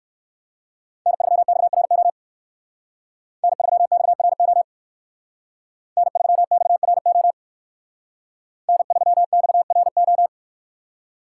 Morse code, heard as a keyed tone, about 58 wpm, 700 Hz. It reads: N3XRO N3XRO N3XRO N3XRO